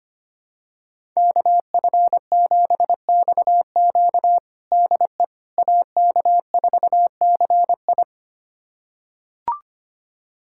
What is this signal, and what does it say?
Morse code, heard as a keyed tone, about 25 wpm, 700 Hz. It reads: KF7XQ DE AK4CI
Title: USA Callsign DE Callsign - 5 Characters - 25wpm